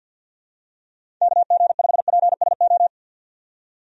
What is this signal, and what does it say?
Morse code, heard as a keyed tone, about 51 wpm, 700 Hz. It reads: KG5PSO
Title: USA Call Signs - 6 Characters - 50wpm